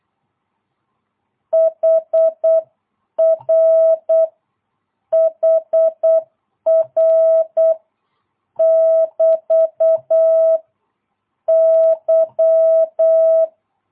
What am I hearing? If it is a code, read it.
Morse code, heard as a keyed tone, about 8 wpm, 660 Hz. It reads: HR HR <BT> Y
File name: jh8cbh / 2022cw